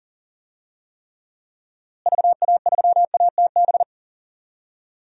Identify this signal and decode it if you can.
Morse code, heard as a keyed tone, about 39 wpm, 700 Hz. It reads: VA3ATB
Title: DXCC-100 Call Signs - 6 Characters - Repeated 3x - 40wpm